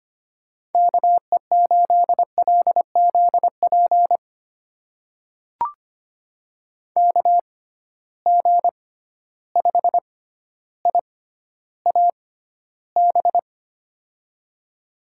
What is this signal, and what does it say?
Morse code, heard as a keyed tone, about 25 wpm, spaced as for 13 wpm, 700 Hz. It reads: KE8LZP KG5IAB
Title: USA Call Signs - 6 Characters - 6x Character Spacing - 25wpm